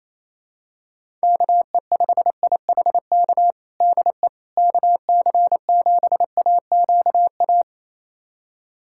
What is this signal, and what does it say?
Morse code, heard as a keyed tone, about 28 wpm, 700 Hz. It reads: KE5IHK DE KC7AQA
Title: USA Callsign DE Callsign - 6 Characters - 28wpm